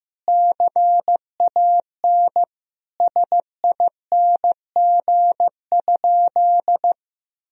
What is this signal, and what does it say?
Morse code, heard as a keyed tone, about 15 wpm, 700 Hz. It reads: CAN SING?
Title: Sentences from Top 1000 Words - Review Part 2 - 15wpm